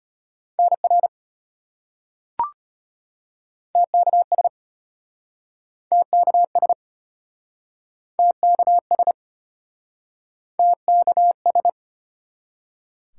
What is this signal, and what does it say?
Morse code, heard as a keyed tone, about 35 wpm, 700 Hz. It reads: NR TKS TKS TKS TKS